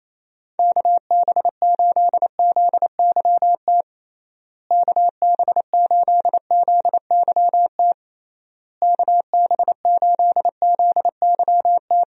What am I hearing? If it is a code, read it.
Morse code, heard as a keyed tone, about 28 wpm, 700 Hz. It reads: KB8ZYT KB8ZYT KB8ZYT
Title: USA Call Signs - 6 Characters - Repeated 3x - 28wpm